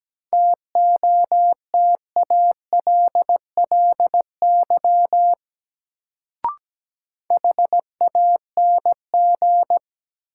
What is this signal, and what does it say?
Morse code, heard as a keyed tone, about 17 wpm, 700 Hz. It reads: TOTALLY HANG
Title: Sentences from Top 1600 Words - New Word Drill Alt - 17wpm